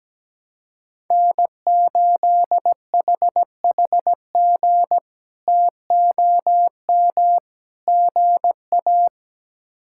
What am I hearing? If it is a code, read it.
Morse code, heard as a keyed tone, about 17 wpm, 700 Hz. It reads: N8HHG TOM GA
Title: K1USN SST QSO Practice - 17wpm